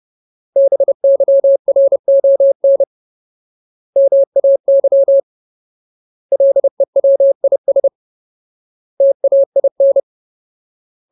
Morse code, heard as a keyed tone, about 30 wpm, 550 Hz. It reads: BYRON MAY LEWIS TAIN
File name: Ultimate Word List / 30WPM Ultimate